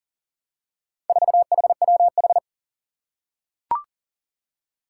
Morse code, heard as a keyed tone, about 38 wpm, 700 Hz. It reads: 4HWH
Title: Four Challenging Letter-Number Combinations 40wpm